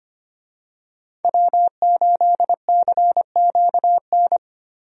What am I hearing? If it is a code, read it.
Morse code, heard as a keyed tone, about 25 wpm, 700 Hz. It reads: W8CQN
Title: USA Call Signs - 5 Characters - 25wpm